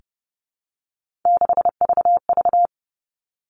Morse code, heard as a keyed tone, about 30 wpm, 700 Hz. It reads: 6VV
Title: All Cut Numbers - 3-Digit 30wpm